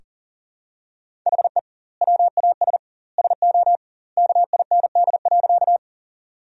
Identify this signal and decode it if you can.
Morse code, heard as a keyed tone, about 40 wpm, 700 Hz. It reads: HE WAS SO KIND.
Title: 2x Word-Spacing - Sentences from Top 200 Words - 40wpm